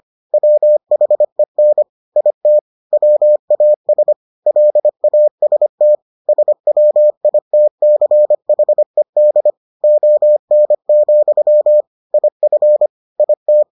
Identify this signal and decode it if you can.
Morse code, heard as a keyed tone, about 25 wpm, 600 Hz. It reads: WHEN IT WAS LAST SWITCHED ON, IF IT